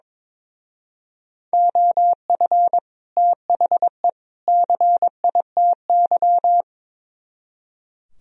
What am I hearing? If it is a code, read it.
Morse code, heard as a keyed tone, about 24 wpm, 700 Hz. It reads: OF THE CITY
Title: Speed-Racing - Sets of 3 Words 22wpm